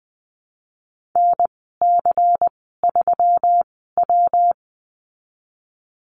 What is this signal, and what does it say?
Morse code, heard as a keyed tone, about 20 wpm, spaced as for 15 wpm, 700 Hz. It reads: NC3W
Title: 122 Lesson 20g - Number 3 - Call Signs 20wpm